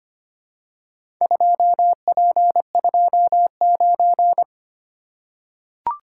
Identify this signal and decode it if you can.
Morse code, heard as a keyed tone, about 25 wpm, 700 Hz. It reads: 2P29
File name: practice / 25wpm / Four Letter-Number Combinations - 25wpm